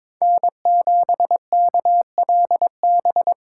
Morse code, heard as a keyed tone, about 26 wpm, 700 Hz. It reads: N7KLB
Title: Speed Racing - USA 1x3 Call Signs 22wpm